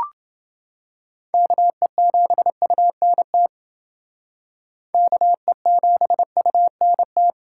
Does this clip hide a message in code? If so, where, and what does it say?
Yes: Morse code from 1.3 to 7.3 s, heard as a keyed tone, about 28 wpm, 700 Hz. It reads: KE7UNT KE7UNT